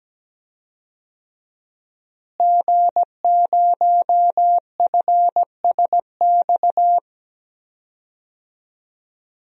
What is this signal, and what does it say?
Morse code, heard as a keyed tone, about 17 wpm, 700 Hz. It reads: G0FSX
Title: DXCC-100 Call Signs - 5 Characters - Repeated 3x - 17wpm